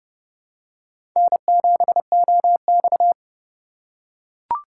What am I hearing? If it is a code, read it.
Morse code, heard as a keyed tone, about 30 wpm, 700 Hz. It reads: N7OX
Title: USA Call Signs - 4x Character Spacing - 30wpm